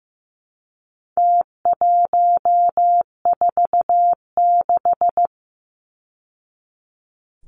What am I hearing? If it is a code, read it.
Morse code, heard as a keyed tone, about 15 wpm, 700 Hz. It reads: T146
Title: Numbers - Contest Style - 4-Digit 15wpm